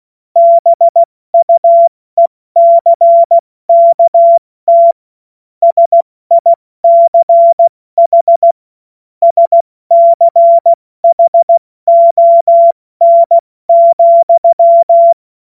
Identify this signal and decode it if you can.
Morse code, heard as a keyed tone, about 16 wpm, 680 Hz. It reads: BUECKT SICH SCHON,
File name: Morsekurs / Witze / Witz 17. August 2021, 70 F80